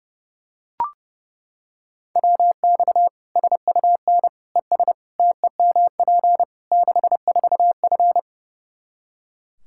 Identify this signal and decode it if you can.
Morse code, heard as a keyed tone, about 32 wpm, 700 Hz. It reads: WX SUN ES TEMP 64F
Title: Practice QSO - Exchange 2 - Info - 30wpm